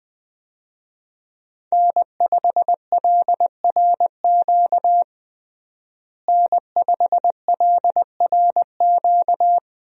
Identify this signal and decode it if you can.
Morse code, heard as a keyed tone, about 20 wpm, 700 Hz. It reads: N5LRQ N5LRQ
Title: USA Call Signs - 5 Characters - Repeated 2x - 20wpm